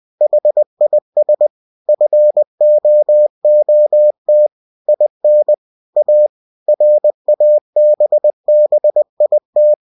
Morse code, heard as a keyed tone, about 20 wpm, 600 Hz. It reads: HIS FOOT IN A RABBIT